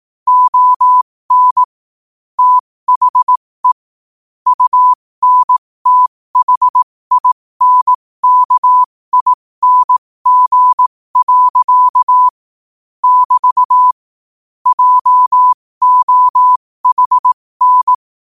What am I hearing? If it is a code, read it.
Morse code, heard as a keyed tone, about 18 wpm, spaced as for 15 wpm, 1000 Hz. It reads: ON THE UNTHINKING. <BT> JOHN
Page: New quotes every day in morse code at 15 Words per minute.